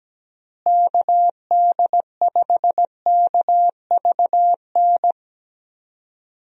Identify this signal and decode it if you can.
Morse code, heard as a keyed tone, about 17 wpm, 700 Hz. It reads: KD5KVN
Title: USA Call Signs - 6 Characters - Repeated 3x - 17wpm